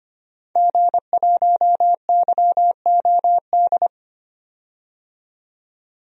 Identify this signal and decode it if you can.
Morse code, heard as a keyed tone, about 25 wpm, 700 Hz. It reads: G1YOD
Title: DXCC-100 Call Signs - 5 Characters - 25wpm